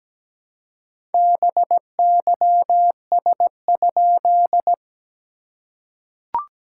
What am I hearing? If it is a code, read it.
Morse code, heard as a keyed tone, about 17 wpm, 690 Hz. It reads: BYS?
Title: SOTA QSO Practice 17wpm